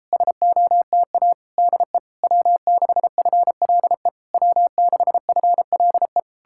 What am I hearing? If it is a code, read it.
Morse code, heard as a keyed tone, about 38 wpm, 700 Hz. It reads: SOTA DE W6FLE W6FLE